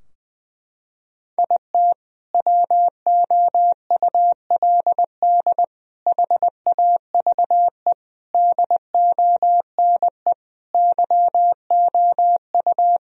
Morse code, heard as a keyed tone, about 21 wpm, 700 Hz. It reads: IT WOULD HAVE DONE YOU
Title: Sentences from Top 1000 Words - Review Part 2 - 20wpm